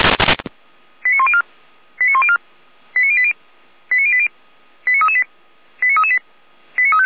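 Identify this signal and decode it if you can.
Five-tone selective call (ZVEI), heard as quick runs of five tones, tones of about 70 ms. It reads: 89104 89104 89080 89080 89338 89338
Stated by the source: Five-tone ZVEI selective call received by SDR, fire brigade Orlamünde (feuerwehr-orlamuende).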